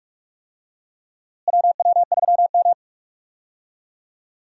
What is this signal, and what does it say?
Morse code, heard as a keyed tone, about 44 wpm, 700 Hz. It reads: WW3M
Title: USA Call Signs - 4 Characters - Repeated 2x - 45wpm